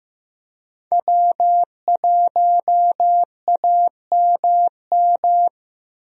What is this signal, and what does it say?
Morse code, heard as a keyed tone, about 15 wpm, 700 Hz. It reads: W1AMM
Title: USA Call Signs - 5 Characters - Repeated 3x - 15wpm